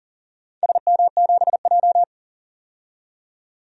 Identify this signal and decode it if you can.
Morse code, heard as a keyed tone, about 37 wpm, 700 Hz. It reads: SM7J
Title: DXCC-100 Call Signs - 4 Characters - 40wpm